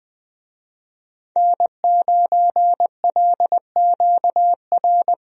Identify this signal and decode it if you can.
Morse code, heard as a keyed tone, about 20 wpm, 700 Hz. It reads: N9LQR
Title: USA Call Signs - 5 Characters - 20wpm